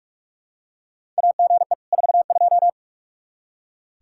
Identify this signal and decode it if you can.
Morse code, heard as a keyed tone, about 46 wpm, 700 Hz. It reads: AGE 42
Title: Practice QSO - Exchange 3 - More Info - 45wpm